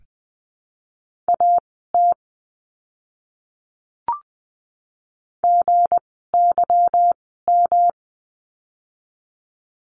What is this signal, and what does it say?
Morse code, heard as a keyed tone, about 20 wpm, spaced as for 14 wpm, 700 Hz. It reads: AT GYM
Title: 160 Lesson 26f - Letter g - Words 20wpm